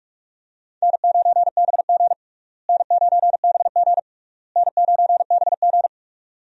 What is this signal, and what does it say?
Morse code, heard as a keyed tone, about 41 wpm, 700 Hz. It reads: N9BG N9BG N9BG
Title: USA Call Signs - 4 Characters - Repeated 3x - 45wpm